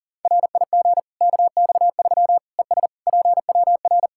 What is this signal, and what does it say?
Morse code, heard as a keyed tone, about 40 wpm, 700 Hz. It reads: RIG KX3 ES PWR
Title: Practice QSO - Exchange 2 - Info - 40wpm